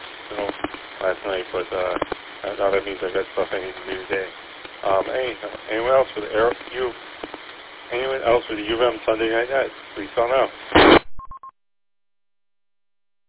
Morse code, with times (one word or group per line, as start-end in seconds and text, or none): none